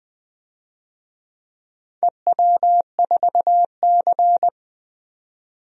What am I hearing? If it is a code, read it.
Morse code, heard as a keyed tone, about 20 wpm, 700 Hz. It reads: EW4C
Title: DXCC-100 Call Signs - 4 Characters - 20wpm